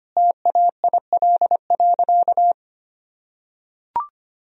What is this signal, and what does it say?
Morse code, heard as a keyed tone, about 25 wpm, 700 Hz. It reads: TAIL.
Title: Sentences from Top 1000 Words - Review Part 3 - 25wpm